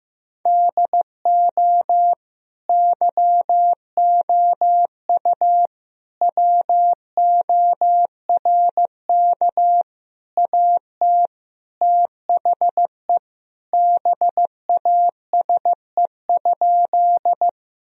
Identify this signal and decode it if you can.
Morse code, heard as a keyed tone, about 15 wpm, 700 Hz. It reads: DO YOU WORK AT THE BASE?
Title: Sentences from Top 1000 Words - Review Part 1 - 15wpm